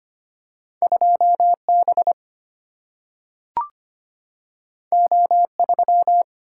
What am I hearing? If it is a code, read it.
Morse code, heard as a keyed tone, about 25 wpm, 700 Hz. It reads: 2 B O 3
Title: Single-Letter Single-Number 25wpm